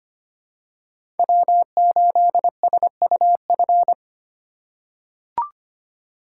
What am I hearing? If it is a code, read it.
Morse code, heard as a keyed tone, about 25 wpm, 700 Hz. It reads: W8SUF
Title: USA Call Signs - 5 Characters - 3x Character Spacing - 25wpm